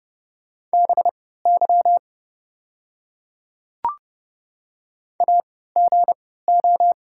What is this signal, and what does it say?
Morse code, heard as a keyed tone, about 30 wpm, spaced as for 17 wpm, 700 Hz. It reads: BY AGO
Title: 3x Character Spacing - Top 500 Words - 30wpm